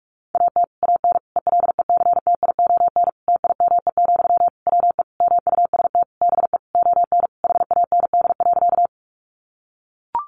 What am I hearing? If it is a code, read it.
Morse code, heard as a keyed tone, about 44 wpm, 700 Hz. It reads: AT AN ELECTION TIME, WE MUST BE ON HAND.
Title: Sentences from Top 1500 Words 45wpm